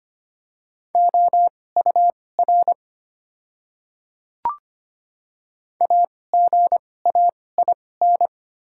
Morse code, heard as a keyed tone, about 25 wpm, spaced as for 18 wpm, 690 Hz. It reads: OUR AGAIN
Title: 2x Character Spacing - Top 200 Words - 25wpm